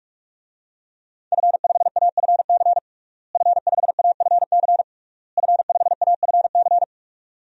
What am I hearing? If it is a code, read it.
Morse code, heard as a keyed tone, about 46 wpm, 700 Hz. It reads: F5AFC F5AFC F5AFC
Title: DXCC-100 Call Signs - 5 Characters - Repeated 3x - 45wpm